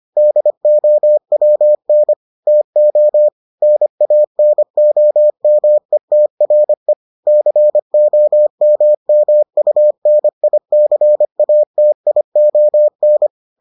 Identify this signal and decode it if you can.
Morse code, heard as a keyed tone, about 25 wpm, 600 Hz. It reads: DOWN TO NANOMETRE COMMUNICATION